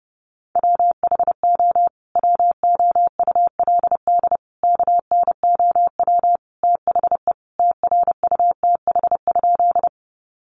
Morse code, heard as a keyed tone, about 30 wpm, 700 Hz. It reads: WHO WOULD KNOW THE TRUTH?
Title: Sentences from Top 800 Words 30wpm